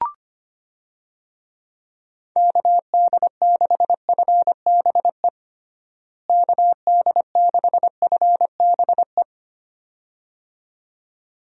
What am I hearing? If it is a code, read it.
Morse code, heard as a keyed tone, about 25 wpm, 700 Hz. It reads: KD6FBE KD6FBE
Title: USA Call Signs - 6 Characters - Repeated 2x - 25wpm